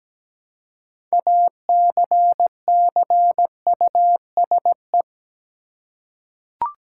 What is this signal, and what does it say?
Morse code, heard as a keyed tone, about 17 wpm, 700 Hz. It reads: ACCUSE
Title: Six Letter Words - 17wpm